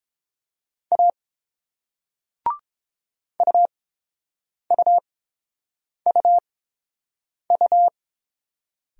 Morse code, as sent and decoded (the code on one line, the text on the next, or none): .- / ..- ..- ..- ..-
A UUUU